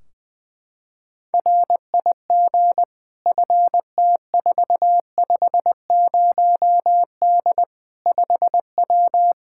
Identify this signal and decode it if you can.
Morse code, heard as a keyed tone, about 20 wpm, 700 Hz. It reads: RIG FT450D 5W
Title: Practice QSO - Exchange 2 - Info - 20wpm